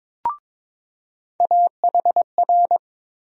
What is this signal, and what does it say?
Morse code, heard as a keyed tone, about 22 wpm, 700 Hz. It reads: AHR
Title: USA Call Sign Postfixes - 3 Letters 22wpm